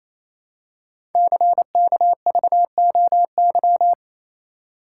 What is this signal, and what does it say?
Morse code, heard as a keyed tone, about 28 wpm, 700 Hz. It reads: CKVOY
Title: Code Groups - 5 Letters - 28wpm